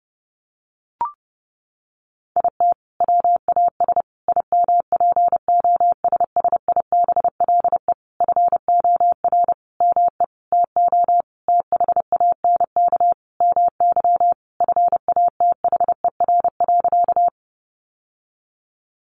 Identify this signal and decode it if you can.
Morse code, heard as a keyed tone, about 30 wpm, 700 Hz. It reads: IT WAS IMPOSSIBLE FOR ME TO THANK MY FATHER.
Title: Sentences from Top 1100 Words 30wpm